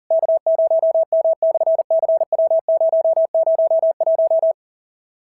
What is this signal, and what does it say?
Morse code, heard as a keyed tone, about 40 wpm, 650 Hz. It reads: K0M/CW001
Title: SOTA QSO Practice 40wpm